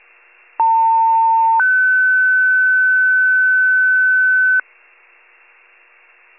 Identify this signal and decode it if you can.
Two-tone page, heard as a one-second tone, then a three-second tone, about 900 then 1500 Hz. It is heard from 0.6 to 4.6 s.